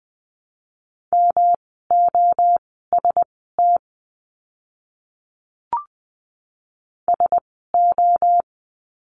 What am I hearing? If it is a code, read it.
Morse code, heard as a keyed tone, about 20 wpm, spaced as for 14 wpm, 700 Hz. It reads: MOST SO
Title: 237 Lesson 38f - Letter q - Words 20wpm